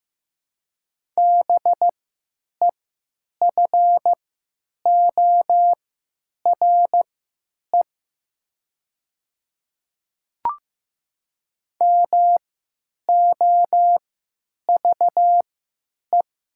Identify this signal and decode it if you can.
Morse code, heard as a keyed tone, about 15 wpm, spaced as for 9 wpm, 700 Hz. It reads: BEFORE MOVE
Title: 3x Character Spacing - Top 200 Words - 15wpm